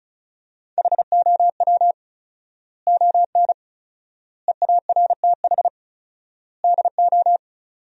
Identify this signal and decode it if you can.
Morse code, heard as a keyed tone, about 34 wpm, 700 Hz. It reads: HOW ON EARTH DO
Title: 4x Word-Spacing - Sentences from Top 200 Words - 35wpm